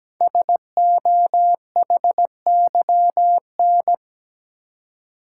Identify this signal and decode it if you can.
Morse code, heard as a keyed tone, about 17 wpm, 700 Hz. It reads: SOHYN
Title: Code Groups - 5 Letters - 17wpm